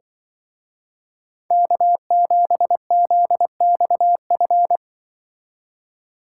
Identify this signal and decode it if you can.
Morse code, heard as a keyed tone, about 23 wpm, 700 Hz. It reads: K7ZXF
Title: Speed Racing - USA Call Signs 20wpm